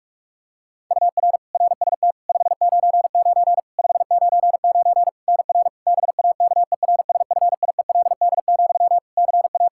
Morse code, heard as a keyed tone, about 46 wpm, 700 Hz. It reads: UR RST 599 599 NR BAKERSFIELD, CA
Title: Practice QSO - Exchange 1 - Beginning - 45wpm